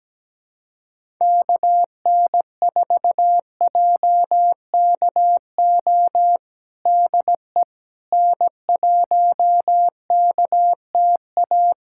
Morse code, heard as a keyed tone, about 17 wpm, 700 Hz. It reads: KN4JKO DE N1KTA
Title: USA Callsign DE Callsign 17wpm